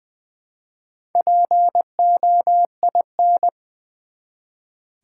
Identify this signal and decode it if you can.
Morse code, heard as a keyed tone, about 20 wpm, 700 Hz. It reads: POIN
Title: Build A Word - 1 Letter At A Time - 3 to 5-Letter Words - 20wpm